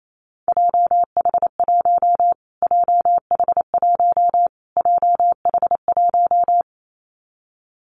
Morse code, heard as a keyed tone, about 28 wpm, 700 Hz. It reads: JH1 JH1 JH1
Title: Callsign Prefixes for Top 25 DXCC - Repeated 3x - 28wpm